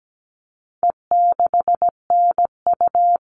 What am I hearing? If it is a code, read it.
Morse code, heard as a keyed tone, about 17 wpm, 700 Hz. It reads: E6NU
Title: All Cut Numbers - 4-Digit 17wpm